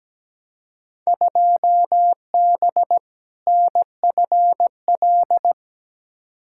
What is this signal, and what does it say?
Morse code, heard as a keyed tone, about 17 wpm, 700 Hz. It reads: 2B NFL
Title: Field Day Exchanges Encore - 17wpm